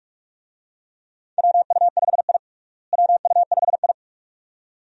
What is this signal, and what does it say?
Morse code, heard as a keyed tone, about 46 wpm, 700 Hz. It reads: WU5I WU5I
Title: USA Call Signs - 4 Characters - Repeated 2x - 45wpm